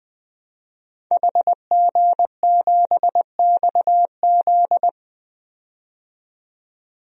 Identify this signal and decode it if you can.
Morse code, heard as a keyed tone, about 20 wpm, 700 Hz. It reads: HG7XZ
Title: DXCC-100 Call Signs - 5 Characters - 20wpm